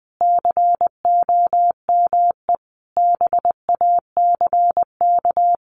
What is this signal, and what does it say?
Morse code, heard as a keyed tone, about 20 wpm, 700 Hz. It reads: COME BACK
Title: Sentences from Top 1500 Words 20wpm